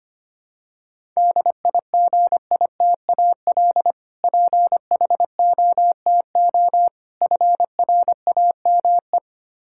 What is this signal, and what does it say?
Morse code, heard as a keyed tone, about 25 wpm, 700 Hz. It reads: DIGITAL PHOTO FRAME
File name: practice / 25wpm / Difficult Words with Hints - 25wpm